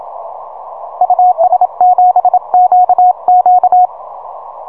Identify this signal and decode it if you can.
Morse code, heard as a keyed tone, about 27 wpm, 730 Hz. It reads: US7QQ
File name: simple-file-list / UN3GX 6M